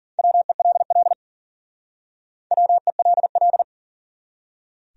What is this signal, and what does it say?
Morse code, heard as a keyed tone, about 46 wpm, 700 Hz. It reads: WELL WELL
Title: Speed-Racing Top 200 Words 40wpm